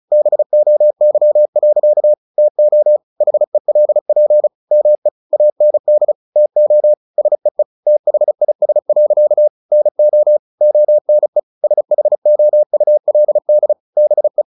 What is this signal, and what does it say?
Morse code, heard as a keyed tone, about 35 wpm, 600 Hz. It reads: BOY. TO HELP ME AND TO SEE THIS. NO ONE SHOULD BE
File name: the old man and the sea / the old man and the sea 1 35wpm